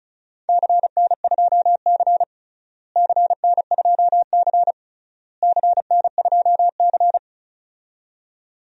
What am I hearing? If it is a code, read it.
Morse code, heard as a keyed tone, about 34 wpm, 700 Hz. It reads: CN2C CN2C CN2C
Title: DXCC-100 Call Signs - 4 Characters - Repeated 3x - 35wpm